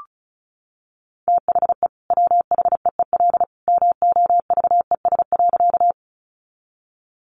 Morse code, heard as a keyed tone, about 34 wpm, 700 Hz. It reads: THE WHEEL MOVES.
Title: Additional Easy Phrases 35wpm